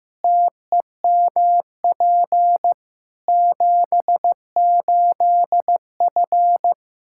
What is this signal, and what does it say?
Morse code, heard as a keyed tone, about 16 wpm, 700 Hz. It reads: TEMP 78F
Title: K1USN SST QSO Practice 15wpm